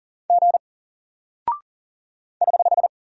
Morse code, heard as a keyed tone, about 40 wpm, 700 Hz. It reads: G <HH>
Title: Single Character (Letter:Number:Punctuation:Prosign) Alt 40wpm